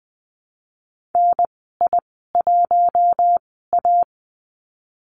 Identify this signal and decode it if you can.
Morse code, heard as a keyed tone, about 20 wpm, spaced as for 14 wpm, 700 Hz. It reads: NI1A
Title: 069 Lesson 12g - Letter d - Call Signs 20wpm